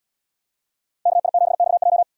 Morse code, heard as a keyed tone, about 71 wpm, 700 Hz. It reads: KI7XJ